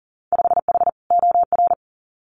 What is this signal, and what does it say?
Morse code, heard as a keyed tone, about 40 wpm, 700 Hz. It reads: 5H OR
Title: Winter Field Day Exchanges Encore 40wpm